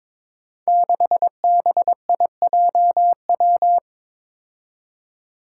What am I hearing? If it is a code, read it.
Morse code, heard as a keyed tone, about 22 wpm, 700 Hz. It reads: 6BIJW
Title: Five Challenging Letter-Number Combinations 22wpm